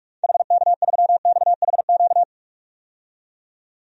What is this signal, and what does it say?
Morse code, heard as a keyed tone, about 46 wpm, 700 Hz. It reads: HK3XHQ
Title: DXCC-100 Call Signs - 6 Characters - 45wpm